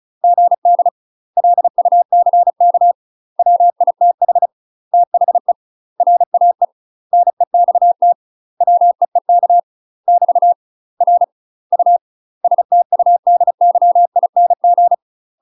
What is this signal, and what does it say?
Morse code, heard as a keyed tone, about 34 wpm, 700 Hz. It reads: GD LUCK WITH THE RAE NEXT WEEK <BT> R U STUDYING
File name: UK MASTER QSOs TRAINING/35WPM